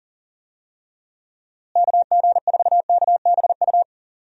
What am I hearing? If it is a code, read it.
Morse code, heard as a keyed tone, about 40 wpm, 700 Hz. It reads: KG4KBU
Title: USA Call Signs - 6 Characters - Repeated 2x - 40wpm